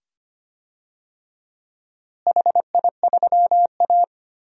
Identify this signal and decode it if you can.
Morse code, heard as a keyed tone, about 25 wpm, 700 Hz. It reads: HI3A
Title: DXCC-100 Call Signs - 4 Characters - Repeated 2x - 25wpm